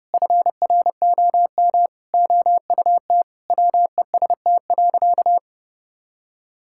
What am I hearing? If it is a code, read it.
Morse code, heard as a keyed tone, about 32 wpm, 700 Hz. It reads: FROM OUT WEST.
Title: Sentences from Top 1000 Words - Review Part 5 - 30wpm